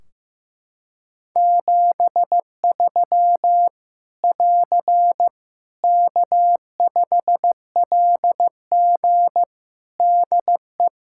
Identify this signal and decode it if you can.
Morse code, heard as a keyed tone, about 15 wpm, 700 Hz. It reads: 73 <AR> K5LG DE
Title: K1USN SST QSO Practice 15wpm